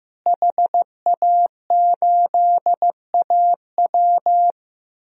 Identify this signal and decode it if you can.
Morse code, heard as a keyed tone, about 15 wpm, 700 Hz. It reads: HA8AW
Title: DXCC-100 Call Signs - 5 Characters - Repeated 3x - 15wpm